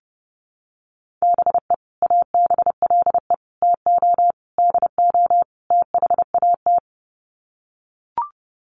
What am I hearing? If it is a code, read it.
Morse code, heard as a keyed tone, about 30 wpm, 700 Hz. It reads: BE ABLE TO DO THAT
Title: Sets of 5 Words Encore2 30wpm